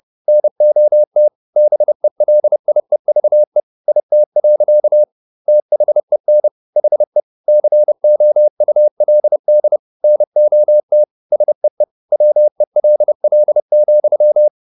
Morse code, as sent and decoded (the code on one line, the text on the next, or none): -. --- - / -... .-.. .. ...- . / .. - .-.-.- / - .... -. / .... / -.-. --- ..- .-.. -.. / -. --- - / ... . / .-- . .-.. .-.. --..--
NOT BELIEVE IT. THEN HE COULD NOT SEE WELL,